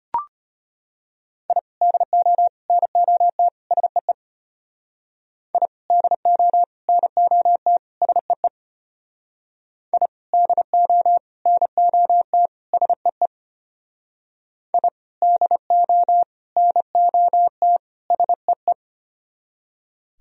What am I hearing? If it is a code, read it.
Morse code, heard as a keyed tone, about 33 wpm, 700 Hz. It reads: I DO NOT SEE I DO NOT SEE I DO NOT SEE I DO NOT SEE